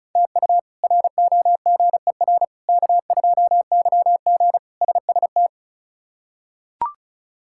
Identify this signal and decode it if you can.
Morse code, heard as a keyed tone, about 36 wpm, 700 Hz. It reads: TU ROGER K2YG SST
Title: K1USN SST QSO Practice - 35wpm